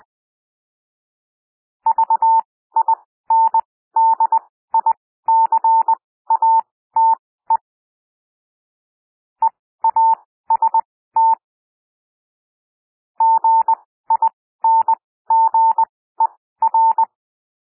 Morse code, heard as a keyed tone, about 20 wpm, spaced as for 14 wpm, 900 Hz. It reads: VINDICATE EAST GINGER
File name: SOUND WORDS 20 wpm